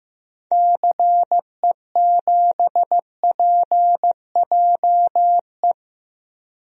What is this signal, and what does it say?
Morse code, heard as a keyed tone, about 15 wpm, 700 Hz. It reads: CE7PJE
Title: DXCC-100 Call Signs - 6 Characters - 15wpm